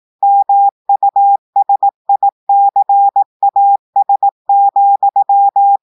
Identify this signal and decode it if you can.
Morse code, heard as a keyed tone, about 18 wpm, 800 Hz. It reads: MUSICAS,
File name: audios / portugais / portugais texte en clair 2 vitesse 18 mots minutes